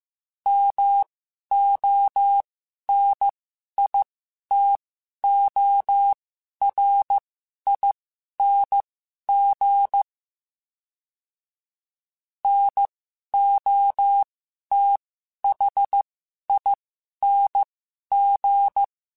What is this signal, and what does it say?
Morse code, heard as a keyed tone, about 15 wpm, spaced as for 11 wpm, 800 Hz. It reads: MONITORING NOTHING
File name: Special INGs-mp3s / Special INGs-mp3s / Sngl Ing @10